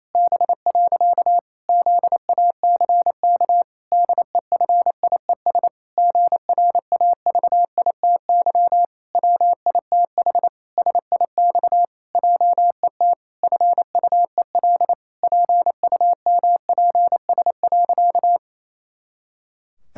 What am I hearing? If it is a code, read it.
Morse code, heard as a keyed tone, about 28 wpm, 700 Hz. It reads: B. ZACK DEFIES GRAVITY WITH SIX JET FUEL PUMPS.
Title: Pangram Challenge v2 - 28wpm